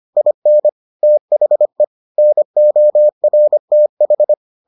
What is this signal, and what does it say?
Morse code, heard as a keyed tone, about 25 wpm, 600 Hz. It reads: IN THE NORTH